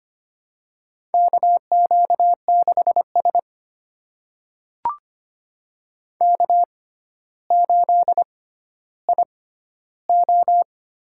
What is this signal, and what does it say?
Morse code, heard as a keyed tone, about 25 wpm, spaced as for 13 wpm, 700 Hz. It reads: KQ6S K8IO
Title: USA Call Signs - 4 Characters - 6x Character Spacing - 25wpm